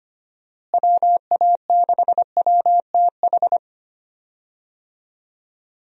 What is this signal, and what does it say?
Morse code, heard as a keyed tone, about 25 wpm, 700 Hz. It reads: WA6WTH
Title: USA Call Signs - 6 Characters - Repeated 3x - 25wpm